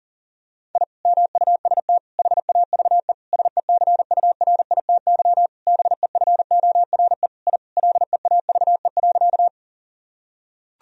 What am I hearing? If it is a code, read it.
Morse code, heard as a keyed tone, about 37 wpm, 700 Hz. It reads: I MUST HAVE SECURITY BEFORE I LEAVE.
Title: Sentences from Top 1000 Words - Review Part 4 - 40wpm